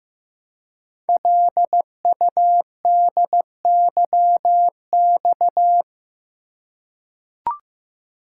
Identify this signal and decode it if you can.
Morse code, heard as a keyed tone, about 15 wpm, 700 Hz. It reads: LUDYX
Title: Code Groups - 5 Letters - 15wpm